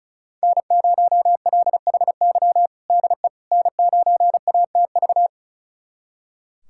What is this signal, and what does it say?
Morse code, heard as a keyed tone, about 36 wpm, 700 Hz. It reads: N0LHY DE N9ATV
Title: USA Callsign DE Callsign - 5 Characters - 35wpm